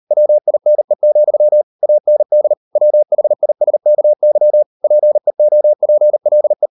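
Morse code, heard as a keyed tone, about 40 wpm, 600 Hz. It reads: WINE, AND WHISKY PEOPLE
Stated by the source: Morse code practice at 38 WPM, sending Ken Follett's novel Never in English, from CW med Gnister.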